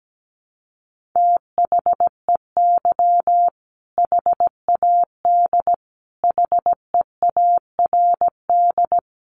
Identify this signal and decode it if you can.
Morse code, heard as a keyed tone, about 17 wpm, 700 Hz. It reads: THEY HAD HEARD
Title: Sentences from Top 400 Words 17wpm